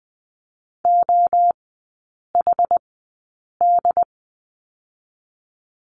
Morse code, heard as a keyed tone, about 20 wpm, spaced as for 8 wpm, 700 Hz. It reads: OHD
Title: 067 Lesson 12e - Letter d - Three Characters 20wpm